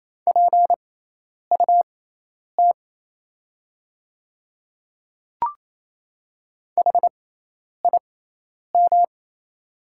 Morse code, heard as a keyed tone, about 27 wpm, spaced as for 10 wpm, 700 Hz. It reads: PUT HIM
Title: 6x Character Spacing - Top 300 Words - 28wpm